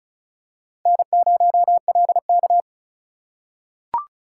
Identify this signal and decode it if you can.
Morse code, heard as a keyed tone, about 36 wpm, 700 Hz. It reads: N0LK
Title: USA Call Signs - 4 Characters - 4x Character Spacing - 35wpm